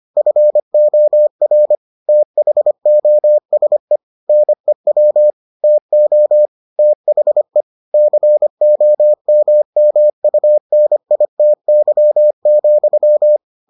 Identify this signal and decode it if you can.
Morse code, heard as a keyed tone, about 25 wpm, 600 Hz. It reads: FOR THOSE NEW TO THE COMMUNITY,